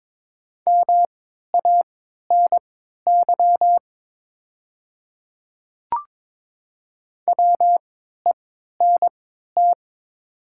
Morse code, heard as a keyed tone, about 22 wpm, spaced as for 12 wpm, 690 Hz. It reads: MANY WENT
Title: 3x Character Spacing - Top 200 Words - 22wpm